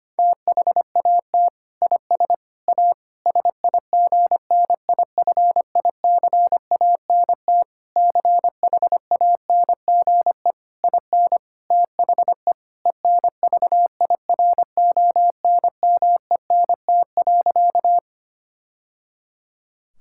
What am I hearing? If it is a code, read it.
Morse code, heard as a keyed tone, about 25 wpm, 700 Hz. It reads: THAT IS A SIGNIFICANT CHANGE IN THE ENVIRONMENT.
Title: Sentences from Top 1000 Words - Review Part 6 - 25wpm